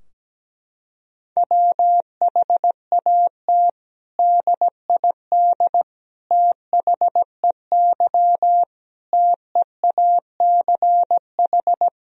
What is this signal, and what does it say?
Morse code, heard as a keyed tone, about 17 wpm, 700 Hz. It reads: WHAT DID THEY TEACH
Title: Sentences from Top 1000 Words - Review Part 2 - 17wpm